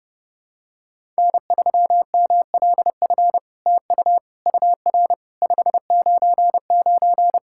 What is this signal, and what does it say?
Morse code, heard as a keyed tone, about 30 wpm, 700 Hz. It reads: N3MLF TU UR 599
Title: SOTA QSO Practice 30wpm